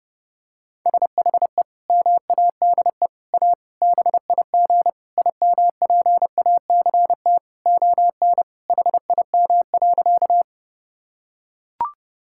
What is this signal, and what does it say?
Morse code, heard as a keyed tone, about 30 wpm, 700 Hz. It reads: SHE MADE A BIG IMPACT ON HIM.
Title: Sentences from Top 1000 Words - Review Part 6 - 30wpm